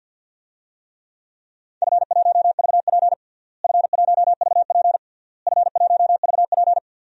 Morse code, heard as a keyed tone, about 52 wpm, 700 Hz. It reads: F1VP F1VP F1VP
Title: DXCC-100 Call Signs - 4 Characters - Repeated 3x - 50wpm